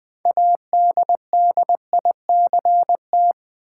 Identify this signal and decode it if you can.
Morse code, heard as a keyed tone, about 20 wpm, 700 Hz. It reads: ADDICT
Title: Build A Word - 3 Letters At A Time - 9-Letter Words - 20wpm